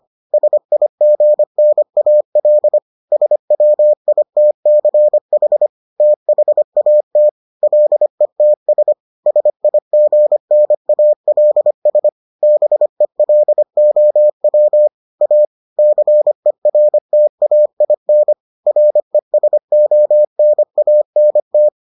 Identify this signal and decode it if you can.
Morse code, heard as a keyed tone, about 25 wpm, 600 Hz. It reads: SIGNAL SWITCH THAT LETS SIGNALS BELOW A CERTAIN RESONANT